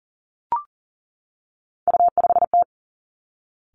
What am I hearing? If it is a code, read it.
Morse code, heard as a keyed tone, about 40 wpm, 700 Hz. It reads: U5T
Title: Three Challenging Letter-Number Combinations 40wpm